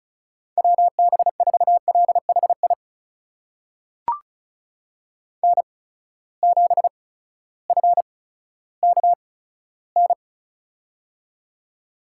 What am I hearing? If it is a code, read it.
Morse code, heard as a keyed tone, about 36 wpm, spaced as for 16 wpm, 700 Hz. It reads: WB4LHI N7FKN
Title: USA Call Signs - 8x Character Spacing - 35wpm